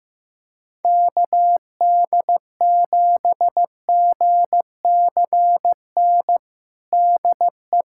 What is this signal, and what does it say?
Morse code, heard as a keyed tone, about 15 wpm, 700 Hz. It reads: KD7GCN DE
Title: USA Callsign DE Callsign 15wpm